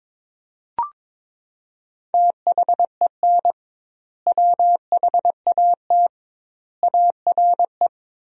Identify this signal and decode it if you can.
Morse code, heard as a keyed tone, about 22 wpm, 700 Hz. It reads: THEN WHAT ARE
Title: 2x Word-Spacing - Sentences from Top 100 Words - 22wpm